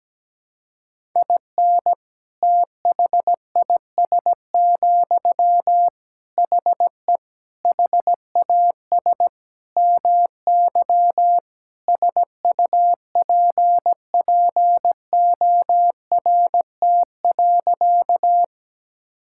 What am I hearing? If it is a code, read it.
Morse code, heard as a keyed tone, about 17 wpm, 700 Hz. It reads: IN THIS, HE HAS MY SUPPORT.
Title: Sentences from Top 1000 Words - Review Part 5 - 17wpm